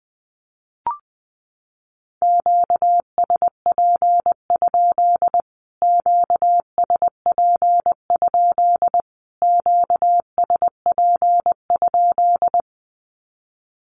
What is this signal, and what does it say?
Morse code, heard as a keyed tone, about 20 wpm, 700 Hz. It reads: QSP? QSP? QSP?
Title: Q-Codes Repeated 3x 20wpm